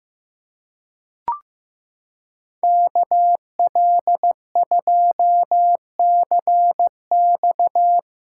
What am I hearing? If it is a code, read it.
Morse code, heard as a keyed tone, about 16 wpm, 700 Hz. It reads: KL2CX
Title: USA Callsign DE Callsign - 5 Characters - 15wpm